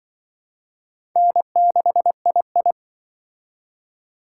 Morse code, heard as a keyed tone, about 24 wpm, 700 Hz. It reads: N6II
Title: Speed Racing - USA 1x2 Call Signs 20wpm